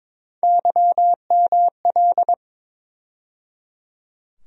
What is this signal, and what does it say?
Morse code, heard as a keyed tone, about 22 wpm, 700 Hz. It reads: YML
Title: USA Call Sign Postfixes - 3 Letters 22wpm